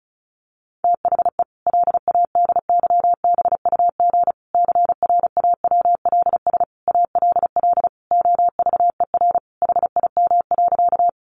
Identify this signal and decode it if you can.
Morse code, heard as a keyed tone, about 34 wpm, 700 Hz. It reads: THE LADYBUG CRAWLS ALL OVER HIM.
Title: Additional Easy Phrases 35wpm